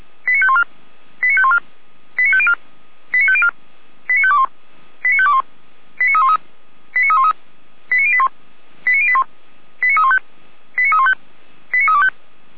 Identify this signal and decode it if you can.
Five-tone selective call (ZVEI), heard as quick runs of five tones, tones of about 70 ms. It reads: 89514 89514 89504 89504 89521 89521 89313 89313 89081 89081 89315 89315 89325